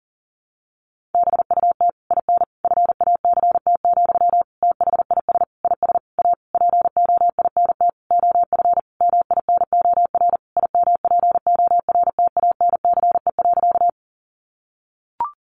Morse code, heard as a keyed tone, about 40 wpm, 700 Hz. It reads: BUT IN FACT, THIS IS A POINT OF MINOR IMPORTANCE.
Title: Sentences from Top 1500 Words 40wpm